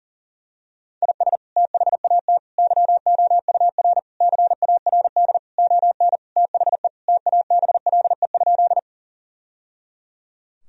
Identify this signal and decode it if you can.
Morse code, heard as a keyed tone, about 37 wpm, 700 Hz. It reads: IS THAT YOUR CARD ON THE TABLE?
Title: Sentences from Top 1000 Words - Review Part 4 - 40wpm